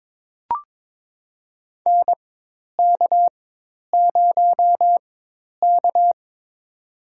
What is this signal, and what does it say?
Morse code, heard as a keyed tone, about 22 wpm, spaced as for 10 wpm, 700 Hz. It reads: NK0K
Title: USA Call Signs - 4x Character Spacing - 22wpm